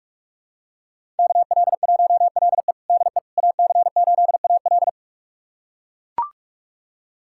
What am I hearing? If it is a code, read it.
Morse code, heard as a keyed tone, about 44 wpm, 700 Hz. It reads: KL1LE DE AC8AL
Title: USA Callsign DE Callsign - 5 Characters - 45wpm